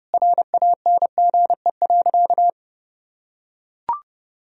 Morse code, heard as a keyed tone, about 30 wpm, 700 Hz. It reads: RANGE.
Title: Sentences from Top 1000 Words - Review Part 4 - 30wpm